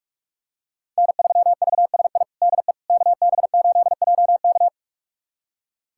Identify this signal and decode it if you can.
Morse code, heard as a keyed tone, about 46 wpm, 700 Hz. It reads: N3VSI DE KB8JK
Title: USA Callsign DE Callsign - 5 Characters - 45wpm